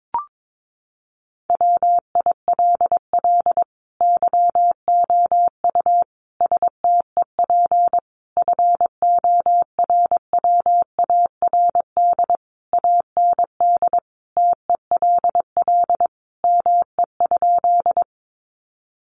Morse code, heard as a keyed tone, about 22 wpm, 700 Hz. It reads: WILL YOU STEP FORWARD AND TELL ME?
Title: Sentences from Top 1300 Words 22wpm